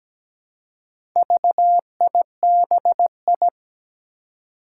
Morse code, heard as a keyed tone, about 17 wpm, 700 Hz. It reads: VIBI
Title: Four Challenging Letter-Number Combinations 17wpm